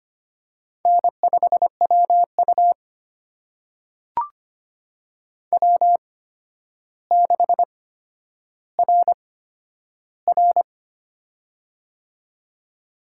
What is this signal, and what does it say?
Morse code, heard as a keyed tone, about 25 wpm, spaced as for 11 wpm, 700 Hz. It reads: N5WU W6RR
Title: USA Call Signs - 4 Characters - 8x Character Spacing - 25wpm